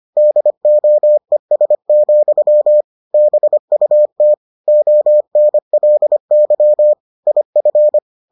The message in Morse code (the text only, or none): DOES, BUT ONLY IF